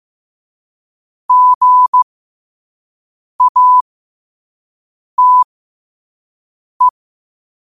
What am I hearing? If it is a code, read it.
Morse code, heard as a keyed tone, about 15 wpm, spaced as for 5 wpm, 1000 Hz. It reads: GATE
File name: qotdfiles / 5wpm